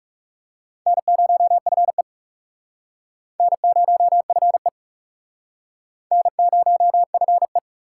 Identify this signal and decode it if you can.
Morse code, heard as a keyed tone, about 37 wpm, 700 Hz. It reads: N0FE N0FE N0FE
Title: Speed Racing - USA 1x2 Call Signs 30wpm